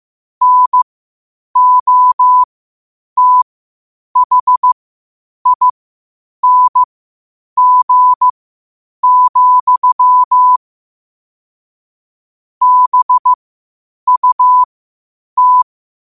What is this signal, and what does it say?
Morse code, heard as a keyed tone, about 15 wpm, spaced as for 8 wpm, 1000 Hz. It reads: NOTHING, BUT
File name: qotdfiles / 8wpm